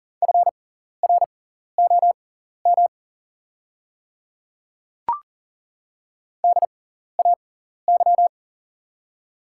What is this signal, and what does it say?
Morse code, heard as a keyed tone, about 40 wpm, spaced as for 14 wpm, 690 Hz. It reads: FROM DAY
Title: 6x Character Spacing - Top 100 Words - 40wpm